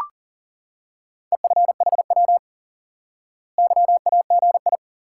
Morse code, heard as a keyed tone, about 40 wpm, 700 Hz. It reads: EFHW YAGI
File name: IQE Course 3 - Lesson 89 - YAGI - Two Elements - 40wpm